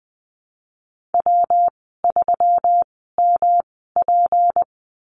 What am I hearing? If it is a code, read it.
Morse code, heard as a keyed tone, about 20 wpm, spaced as for 14 wpm, 700 Hz. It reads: W3MP
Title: 154 Lesson 25g - Letter p - Call Signs 20wpm